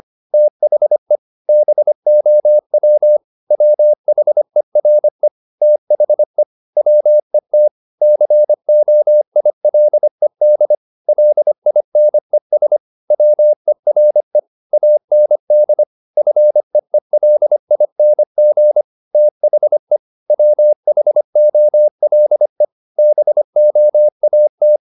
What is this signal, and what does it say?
Morse code, heard as a keyed tone, about 25 wpm, 600 Hz. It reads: THE BOW WHERE THE WET COILED LINES WERE AND FEELING THE WHOLE BOAT
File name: the old man and the sea part 1 25wpm